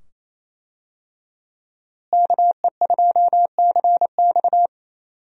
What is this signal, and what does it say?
Morse code, heard as a keyed tone, about 28 wpm, 700 Hz. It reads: KE2CX
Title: USA Call Signs - 5 Characters - Repeated 2x - 28wpm